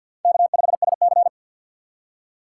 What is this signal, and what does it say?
Morse code, heard as a keyed tone, about 46 wpm, 700 Hz. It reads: K5SC
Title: USA Call Signs - 4 Characters - 3x Character Spacing - 50wpm